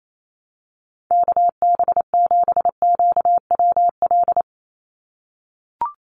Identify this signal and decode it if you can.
Morse code, heard as a keyed tone, about 28 wpm, 700 Hz. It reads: KB7QWL
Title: USA Call Signs 28wpm